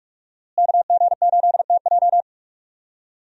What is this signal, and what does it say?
Morse code, heard as a keyed tone, about 46 wpm, 700 Hz. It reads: KG8TJ
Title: USA Call Signs - 5 Characters - Repeated 2x - 45wpm